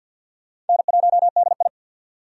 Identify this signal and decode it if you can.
Morse code, heard as a keyed tone, about 51 wpm, 700 Hz. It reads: N1DI
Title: USA Call Signs - 4 Characters - 50wpm